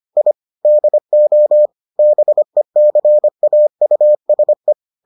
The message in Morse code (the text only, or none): I DO BECAUSE